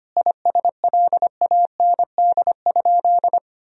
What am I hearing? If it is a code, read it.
Morse code, heard as a keyed tone, about 25 wpm, 700 Hz. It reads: ISLAND?
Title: Sentences from Top 1000 Words - Review Part 3 - 25wpm